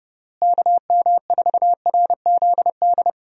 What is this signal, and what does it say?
Morse code, heard as a keyed tone, about 30 wpm, 700 Hz. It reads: KM4RZD
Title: USA Call Signs - 6 Characters - Repeated 2x - 30wpm